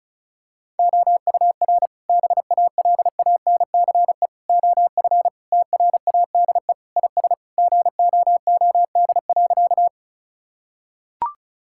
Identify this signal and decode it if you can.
Morse code, heard as a keyed tone, about 36 wpm, 700 Hz. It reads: OUR BALANCE OF TRADE IS GOOD.
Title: Sentences from Top 1000 Words - Review Part 6 - 35wpm